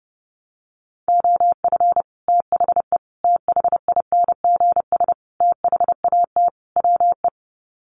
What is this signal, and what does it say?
Morse code, heard as a keyed tone, about 30 wpm, 700 Hz. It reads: OF THE THINGS THAT WE
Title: Sets of 5 Words Encore 30wpm